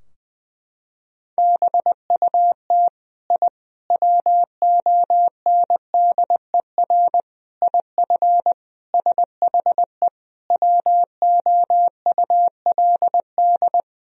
Morse code, heard as a keyed tone, about 21 wpm, 700 Hz. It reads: BUT I WONDER IF SHE WOULD
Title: Sentences from Top 1000 Words - Review Part 3 - 20wpm